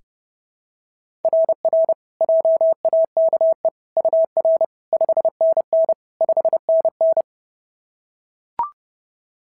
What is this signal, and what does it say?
Morse code, heard as a keyed tone, about 29 wpm, 690 Hz. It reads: RR JAKE UR 5NN 5NN
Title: SOTA QSO Practice 30wpm